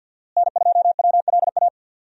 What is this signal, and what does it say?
Morse code, heard as a keyed tone, about 47 wpm, 700 Hz. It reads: N2WLA
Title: USA Call Signs - 5 Characters - Repeated 2x - 50wpm